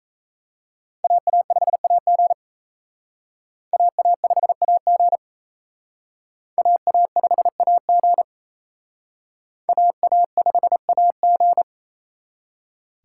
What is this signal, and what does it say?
Morse code, heard as a keyed tone, about 35 wpm, 700 Hz. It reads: AA5AG AA5AG AA5AG AA5AG